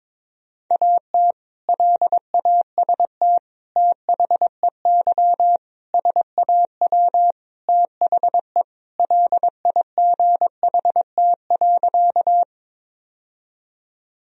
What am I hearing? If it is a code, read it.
Morse code, heard as a keyed tone, about 22 wpm, 700 Hz. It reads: AT LAST THEY SAW THE LIGHT.
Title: Sentences from Top 1000 Words - Review Part 1 - 22wpm